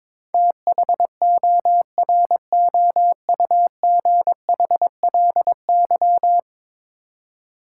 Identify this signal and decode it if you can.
Morse code, heard as a keyed tone, about 22 wpm, 700 Hz. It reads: THOROUGHLY
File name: practice / 22wpm / 10 Letter Words - First 3 Letters Given - 22wpm